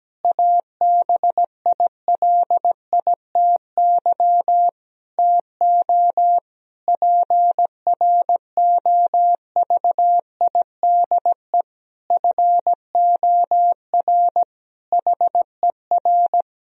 Morse code, heard as a keyed tone, about 17 wpm, 700 Hz. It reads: ABILITY TO PROVIDE FOR HER
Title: Sentences from Top 1000 Words - Review Part 3 - 17wpm